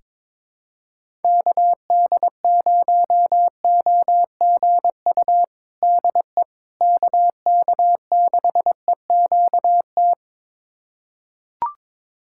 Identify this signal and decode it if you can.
Morse code, heard as a keyed tone, about 22 wpm, 700 Hz. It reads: KD0OGU DE KK6EQT
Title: USA Callsign DE Callsign - 6 Characters - 22wpm